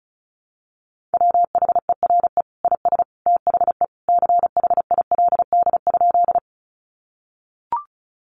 Morse code, heard as a keyed tone, about 34 wpm, 700 Hz. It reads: WHERE IS THE CHILD?
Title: Sentences from Top 800 Words 35wpm